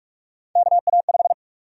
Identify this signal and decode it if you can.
Morse code, heard as a keyed tone, about 44 wpm, 700 Hz. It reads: KA5
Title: USA Call Sign Prefixes - Repeated 2x - 45wpm